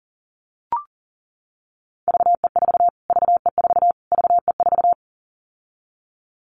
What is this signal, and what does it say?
Morse code, heard as a keyed tone, about 40 wpm, 700 Hz. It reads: VE4 VE4 VE4
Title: Callsign Prefixes for Top 25 DXCC - Repeated 3x - 40wpm